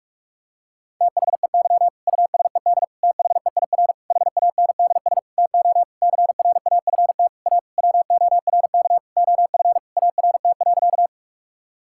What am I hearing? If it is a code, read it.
Morse code, heard as a keyed tone, about 41 wpm, 700 Hz. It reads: THEY USED THEIR HANDS TO CRAFT A WORK OF ART.
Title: Sentences from Top 1000 Words - Review Part 6 - 45wpm